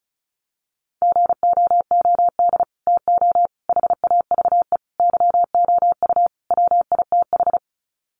Morse code, heard as a keyed tone, about 34 wpm, 700 Hz. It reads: GOOD TO HAVE YOU WITH
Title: Sets of 5 Words Encore2 35wpm